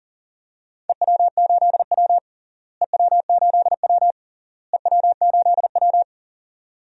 Morse code, heard as a keyed tone, about 40 wpm, 700 Hz. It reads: EW8W EW8W EW8W
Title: DXCC-100 Call Signs - 4 Characters - Repeated 3x - 40wpm